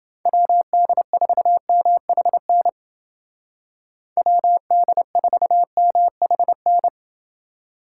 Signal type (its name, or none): Morse code